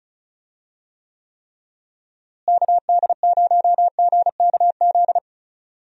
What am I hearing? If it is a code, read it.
Morse code, heard as a keyed tone, about 36 wpm, 700 Hz. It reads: KD0GKZ
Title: USA Call Signs - 6 Characters - Repeated 3x - 35wpm